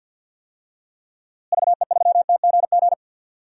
Morse code, heard as a keyed tone, about 50 wpm, 700 Hz. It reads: VE3TGG
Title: DXCC-100 Call Signs - 6 Characters - Repeated 3x - 50wpm